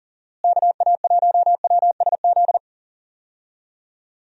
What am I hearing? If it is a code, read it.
Morse code, heard as a keyed tone, about 40 wpm, 700 Hz. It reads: KA1WSZ
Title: USA Call Signs - 6 Characters - Repeated 2x - 40wpm